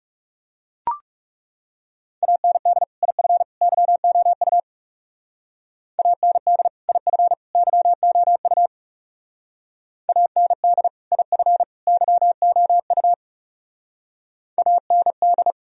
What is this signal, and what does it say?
Morse code, heard as a keyed tone, about 35 wpm, 700 Hz. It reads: AND IF YOU AND IF YOU AND IF YOU AND
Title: Speed-Racing - Sets of 3 Words 30wpm